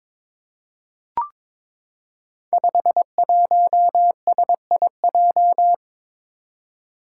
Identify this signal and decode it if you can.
Morse code, heard as a keyed tone, about 21 wpm, 700 Hz. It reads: 51SIJ
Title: Five Challenging Letter-Number Combinations 22wpm